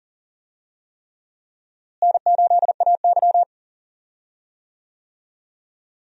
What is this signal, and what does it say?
Morse code, heard as a keyed tone, about 40 wpm, 700 Hz. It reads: N8AY
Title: USA Call Signs - 4 Characters - Repeated 3x - 40wpm